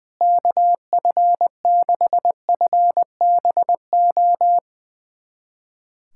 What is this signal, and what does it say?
Morse code, heard as a keyed tone, about 21 wpm, 700 Hz. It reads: KF6FBO
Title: USA Callsign DE Callsign 20wpm